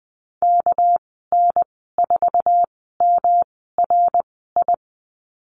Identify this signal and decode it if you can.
Morse code, heard as a keyed tone, about 19 wpm, spaced as for 14 wpm, 700 Hz. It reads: KN4MRI
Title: 238 Lesson 38g - Letter q - Call Signs 20wpm